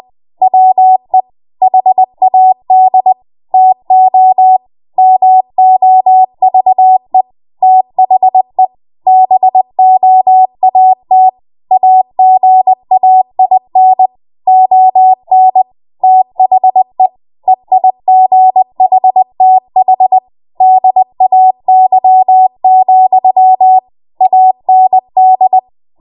Morse code, heard as a keyed tone, about 20 wpm, 750 Hz. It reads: WE HAD TO MOVE THE BOAT AGAIN ON THE EIGHTH DAY, AND
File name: Morse / 20WPM